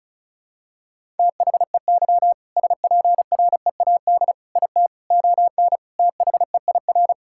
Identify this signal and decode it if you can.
Morse code, heard as a keyed tone, about 34 wpm, 700 Hz. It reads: THEY SPREAD IT ON THEIR
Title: Sentences from Top 1600 Words 35wpm